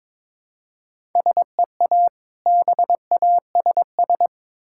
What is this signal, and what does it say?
Morse code, heard as a keyed tone, about 22 wpm, 690 Hz. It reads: SEA BASS
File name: Difficult Words with Hints - 22wpm